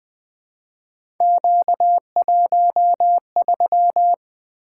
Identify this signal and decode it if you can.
Morse code, heard as a keyed tone, about 20 wpm, 700 Hz. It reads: Q13
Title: Three Letter-Number Combinations - 20wpm